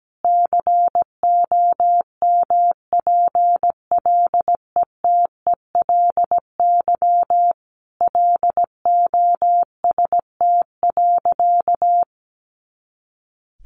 Morse code, heard as a keyed tone, about 17 wpm, 700 Hz. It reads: COMPLETELY LOST.
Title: Additional Easy Phrases 17wpm